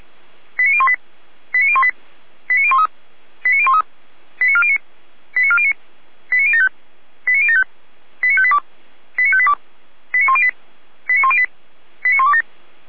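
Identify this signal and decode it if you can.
Five-tone selective call (ZVEI), heard as quick runs of five tones, tones of about 70 ms. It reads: 89018 89018 89013 89013 89409 89409 89075 89075 89582 89582 89108 89108 89127
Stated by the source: ZVEI five-tone selective call received by SDR.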